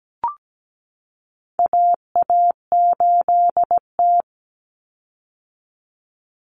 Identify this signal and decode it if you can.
Morse code, heard as a keyed tone, about 17 wpm, 700 Hz. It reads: AA8T
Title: Common Cut Numbers - 4-Digit 17wpm